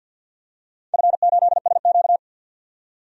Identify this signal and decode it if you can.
Morse code, heard as a keyed tone, about 46 wpm, 700 Hz. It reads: F8SQ
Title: DXCC-100 Call Signs - 4 Characters - 50wpm